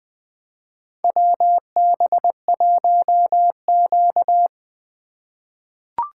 Morse code, heard as a keyed tone, about 20 wpm, 700 Hz. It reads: WB1Q
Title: USA Call Signs - 4 Characters - 3x Character Spacing - 20wpm